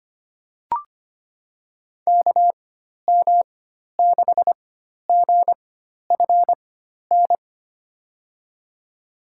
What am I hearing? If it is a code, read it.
Morse code, heard as a keyed tone, about 25 wpm, spaced as for 12 wpm, 700 Hz. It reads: KM6GFN
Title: USA Call Signs - 6 Characters - 4x Character Spacing - 25wpm